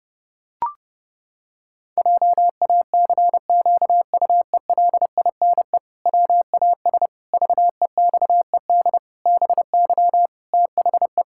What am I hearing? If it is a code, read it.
Morse code, heard as a keyed tone, about 30 wpm, 700 Hz. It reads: JACQUELINE WAS VEXED BY THE
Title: Pangram Challenge v2 - 30wpm